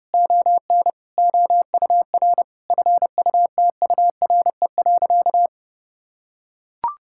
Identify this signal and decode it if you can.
Morse code, heard as a keyed tone, about 28 wpm, 700 Hz. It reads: ON OUR FUTURE.
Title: Sentences from Top 1000 Words - Review Part 6 - 30wpm